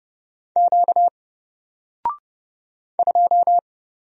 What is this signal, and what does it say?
Morse code, heard as a keyed tone, about 29 wpm, 700 Hz. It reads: Q 2
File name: Single-Letter-Number-30wpm